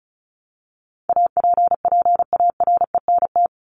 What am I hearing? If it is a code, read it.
Morse code, heard as a keyed tone, about 34 wpm, 700 Hz. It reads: APPARENT
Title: LY Suffix - Top 100 Words with Base Words 35wpm